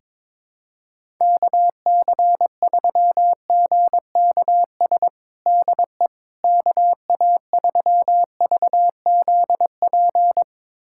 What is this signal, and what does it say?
Morse code, heard as a keyed tone, about 22 wpm, 700 Hz. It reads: KC3GKS DE KA3VZP
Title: USA Callsign DE Callsign - 6 Characters - 22wpm